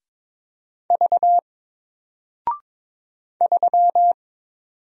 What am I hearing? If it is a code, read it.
Morse code, heard as a keyed tone, about 22 wpm, 700 Hz. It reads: V 3
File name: Single-Letter-Number-22wpm